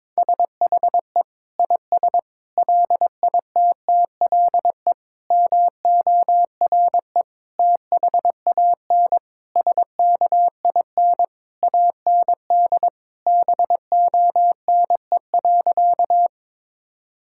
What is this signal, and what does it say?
Morse code, heard as a keyed tone, about 22 wpm, 700 Hz. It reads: SHE IS LITTLE MORE THAN SKIN AND BONE.
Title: Sentences from Top 1600 Words 22wpm